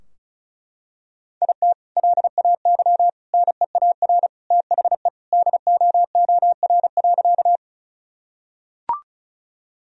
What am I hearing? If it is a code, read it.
Morse code, heard as a keyed tone, about 36 wpm, 700 Hz. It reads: IT LAY NEAR THE DOOR.
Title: Sentences from Top 1000 Words - Review Part 2 - 35wpm